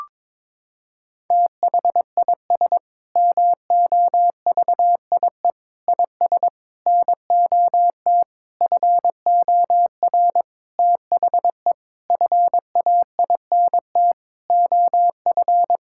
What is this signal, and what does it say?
Morse code, heard as a keyed tone, about 22 wpm, 700 Hz. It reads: THIS MOVIE IS NOT FOR THE FAINT OF
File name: Sentences with 7-Word Phrases - 22wpm